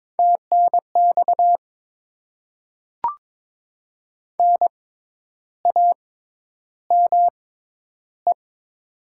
Morse code, heard as a keyed tone, about 22 wpm, spaced as for 10 wpm, 700 Hz. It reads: TNX NAME
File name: Top 20 QSO Elements - 6x Character Spacing - 22wpm